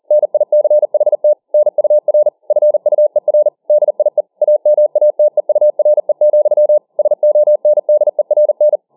Morse code, heard as a keyed tone, about 40 wpm, 600 Hz. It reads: NICHT NUR FUER DIE AMATEURE, SONDERN